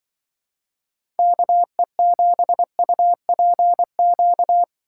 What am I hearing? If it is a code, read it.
Morse code, heard as a keyed tone, about 24 wpm, 700 Hz. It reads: KE7UPQ
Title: Speed Racing - USA 2x3 Call Signs 20wpm